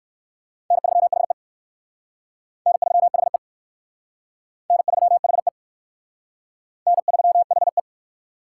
Morse code, heard as a keyed tone, about 54 wpm, 700 Hz. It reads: N3HE N3HE N3HE N3HE
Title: Speed Racing - USA 1x2 Call Signs 45wpm